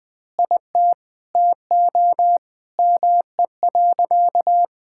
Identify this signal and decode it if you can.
Morse code, heard as a keyed tone, about 20 wpm, 700 Hz. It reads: IT TO ME.
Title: Sentences with 2-Word Phrases - Spoken Beginning - 20wpm